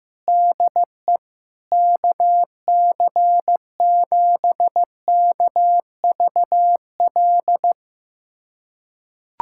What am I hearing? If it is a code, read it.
Morse code, heard as a keyed tone, about 15 wpm, 700 Hz. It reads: DE KC7KVL
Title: USA Callsign DE Callsign - 6 Characters - 15wpm